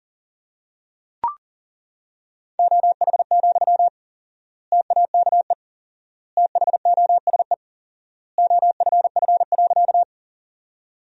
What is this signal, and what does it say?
Morse code, heard as a keyed tone, about 37 wpm, 700 Hz. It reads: OH, TAKE THOSE OFF.
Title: 4x Word-Spacing - Sentences from Top 500 Words - 40wpm